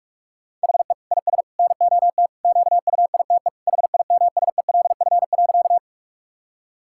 Morse code, heard as a keyed tone, about 41 wpm, 700 Hz. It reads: HE IS NOT QUITE HIMSELF.
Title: Sentences from Top 1600 Words 45wpm